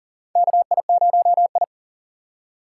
Morse code, heard as a keyed tone, about 40 wpm, 700 Hz. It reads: KI0I
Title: K1USN SST QSO Practice - 40wpm